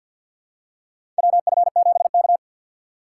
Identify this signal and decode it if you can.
Morse code, heard as a keyed tone, about 46 wpm, 700 Hz. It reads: WF7K
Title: USA Call Signs - 3x Character Spacing - 50wpm